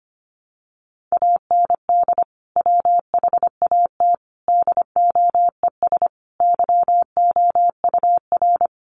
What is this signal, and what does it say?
Morse code, heard as a keyed tone, about 25 wpm, 700 Hz. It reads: AND WHAT DOES YOUR
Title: Sentences from Top 1200 Words 25wpm